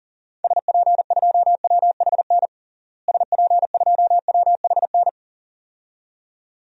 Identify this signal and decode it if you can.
Morse code, heard as a keyed tone, about 40 wpm, 700 Hz. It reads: SP2WHN SP2WHN
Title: DXCC-100 Call Signs - 6 Characters - Repeated 2x - 40wpm